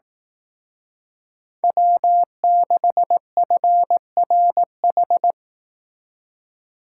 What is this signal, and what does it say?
Morse code, heard as a keyed tone, about 18 wpm, 700 Hz. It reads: W6FRH
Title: Speed Racing - USA 1x3 Call Signs 15wpm